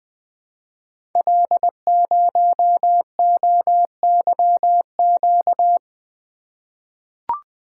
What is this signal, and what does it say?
Morse code, heard as a keyed tone, about 20 wpm, 700 Hz. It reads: L0OYQ
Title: Code Groups - 5 Letter:Numbers - 20wpm